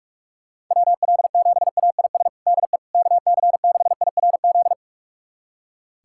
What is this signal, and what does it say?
Morse code, heard as a keyed tone, about 47 wpm, 700 Hz. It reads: WL7AIS DE KC6IRZ
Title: USA Callsign DE Callsign 45wpm